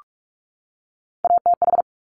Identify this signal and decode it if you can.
Morse code, heard as a keyed tone, about 46 wpm, 700 Hz. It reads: ATH
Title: Top Three Letter Combinations in Words 45wpm